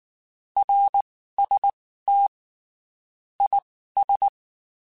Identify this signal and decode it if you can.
Morse code, heard as a keyed tone, about 19 wpm, spaced as for 14 wpm, 800 Hz. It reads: RST IS
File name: (spaced) / QSOs Spaced-mp3s / WD0FBT @18 sp